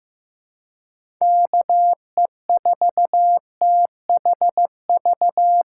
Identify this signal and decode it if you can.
Morse code, heard as a keyed tone, about 15 wpm, 700 Hz. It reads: KE4THV
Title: USA Call Signs - 6 Characters - 2x Character Spacing - 15wpm